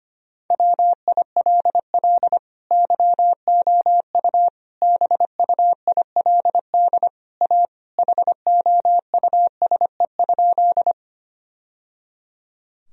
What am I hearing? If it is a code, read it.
Morse code, heard as a keyed tone, about 25 wpm, 700 Hz. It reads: WILL YOU BUILD A HOUSE?
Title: Sentences from Top 1000 Words - Review Part 1 - 25wpm